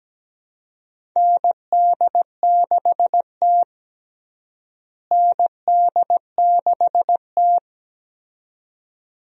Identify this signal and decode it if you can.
Morse code, heard as a keyed tone, about 17 wpm, 700 Hz. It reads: ND6T ND6T
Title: USA Call Signs - 4 Characters - Repeated 2x - 17wpm